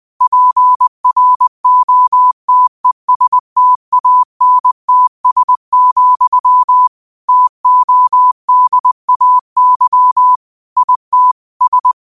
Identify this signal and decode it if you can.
Morse code, heard as a keyed tone, about 20 wpm, 1000 Hz. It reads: PROTESTANTS, TODAY IT S